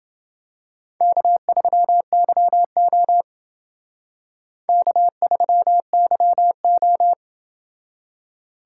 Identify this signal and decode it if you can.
Morse code, heard as a keyed tone, about 28 wpm, 700 Hz. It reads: K3YO K3YO